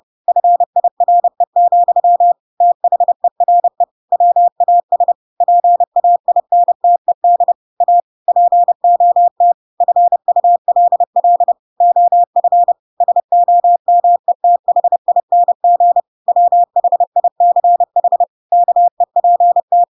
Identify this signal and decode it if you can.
Morse code, heard as a keyed tone, about 30 wpm, 700 Hz. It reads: FIRE, THERE WAS PAINTED A POT FULL OF SOMETHING WHICH KEPT
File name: Pinocchio / Pinocchio ch3 30 30 ik1hge